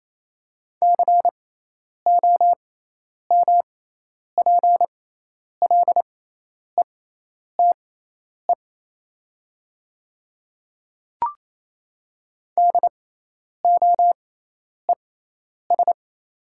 Morse code, heard as a keyed tone, about 28 wpm, spaced as for 10 wpm, 700 Hz. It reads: COMPLETE DOES
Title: 6x Character Spacing - Top 500 Words - 28wpm